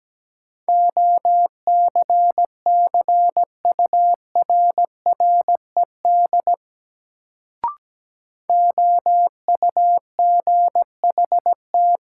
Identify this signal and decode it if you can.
Morse code, heard as a keyed tone, about 17 wpm, 700 Hz. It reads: OCCURRED OUGHT
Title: Sets of 5 Words Encore - New Word Drill - 17wpm